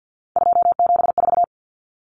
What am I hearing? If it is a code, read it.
Morse code, heard as a keyed tone, about 50 wpm, 700 Hz. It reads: 274
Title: Numbers - 3-Digit 50wpm